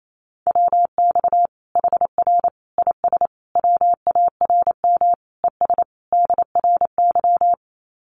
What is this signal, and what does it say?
Morse code, heard as a keyed tone, about 28 wpm, 700 Hz. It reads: WX HR IS WARM ES DRY
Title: Common QSO Phrases 28wpm